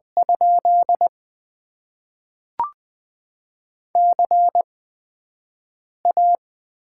Morse code, heard as a keyed tone, about 20 wpm, spaced as for 5 wpm, 700 Hz. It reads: ? CA
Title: Top 90 QSO Elements - 8x Character Spacing - 20wpm